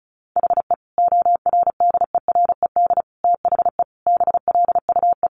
Morse code, heard as a keyed tone, about 34 wpm, 700 Hz. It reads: HE ORDERED THE BLUE
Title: Additional Easy Phrases 35wpm